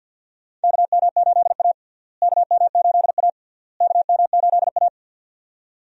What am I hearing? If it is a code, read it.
Morse code, heard as a keyed tone, about 50 wpm, 700 Hz. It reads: KM8A KM8A KM8A
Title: USA Call Signs - 4 Characters - Repeated 3x - 50wpm